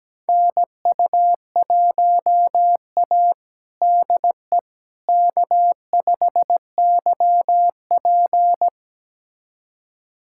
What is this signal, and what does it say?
Morse code, heard as a keyed tone, about 17 wpm, 700 Hz. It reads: NU1A DE K5YP
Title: USA Callsign DE Callsign - 4 Characters - 17wpm